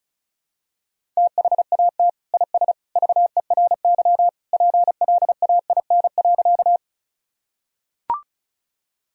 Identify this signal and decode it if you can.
Morse code, heard as a keyed tone, about 34 wpm, 700 Hz. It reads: THAT IS VERY PLAIN.
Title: Sentences from Top 1000 Words - Review Part 2 - 35wpm